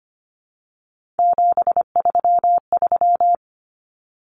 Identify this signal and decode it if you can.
Morse code, heard as a keyed tone, about 25 wpm, 700 Hz. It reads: 733
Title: Common Cut Numbers - 3-Digit 25wpm